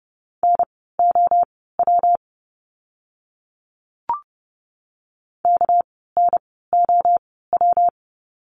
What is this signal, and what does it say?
Morse code, heard as a keyed tone, about 30 wpm, spaced as for 17 wpm, 700 Hz. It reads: NOW KNOW
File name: course / 30:10wpm / 230 Lesson 37f - Letter x - Words 30wpm